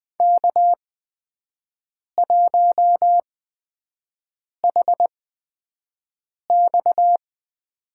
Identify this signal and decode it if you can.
Morse code, heard as a keyed tone, about 20 wpm, spaced as for 5 wpm, 700 Hz. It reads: K1HX
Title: USA Call Signs - 4 Characters - 8x Character Spacing - 20wpm